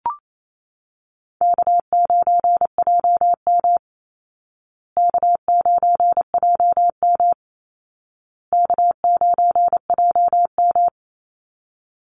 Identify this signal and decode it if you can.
Morse code, heard as a keyed tone, about 28 wpm, 700 Hz. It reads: K9JM K9JM K9JM
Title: Call Signs Repeated 2x - 28wpm